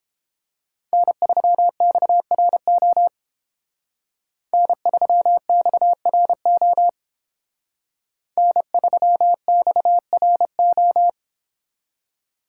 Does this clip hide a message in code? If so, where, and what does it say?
Yes: Morse code from 0.9 to 11.1 s, heard as a keyed tone, about 31 wpm, 700 Hz. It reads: N3XRO N3XRO N3XRO